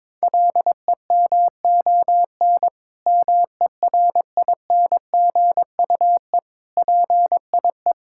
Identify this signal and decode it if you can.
Morse code, heard as a keyed tone, about 22 wpm, 700 Hz. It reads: LEMON MERINGUE PIE
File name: Difficult Words with Hints - 22wpm